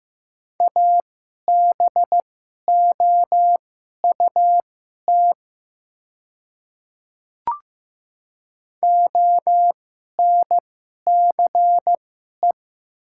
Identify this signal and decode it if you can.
Morse code, heard as a keyed tone, about 15 wpm, spaced as for 11 wpm, 700 Hz. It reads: ABOUT ONCE
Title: 2x Character Spacing - Top 500 Words - 15wpm